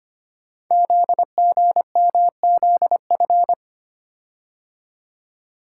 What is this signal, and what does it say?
Morse code, heard as a keyed tone, about 25 wpm, 700 Hz. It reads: ZGMZF
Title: Code Groups - 5 Letters - 25wpm